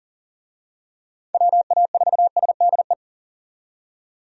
Morse code, heard as a keyed tone, about 40 wpm, 700 Hz. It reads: WA4SDE
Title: USA Call Signs - 6 Characters - 40wpm